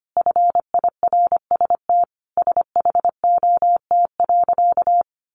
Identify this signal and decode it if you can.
Morse code, heard as a keyed tone, about 25 wpm, 700 Hz. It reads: FIRST SHOT.
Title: A Sound Of Thunder 25wpm